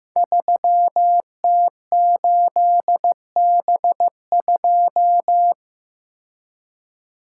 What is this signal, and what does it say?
Morse code, heard as a keyed tone, about 15 wpm, 700 Hz. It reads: 3T8B2
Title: Code Groups - 5 Letter:Numbers - 15wpm